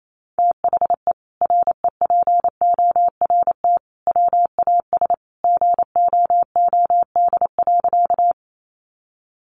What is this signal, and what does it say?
Morse code, heard as a keyed tone, about 28 wpm, 700 Hz. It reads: THE REPORT WAS GOOD.
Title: Sentences from Top 900 Words 28wpm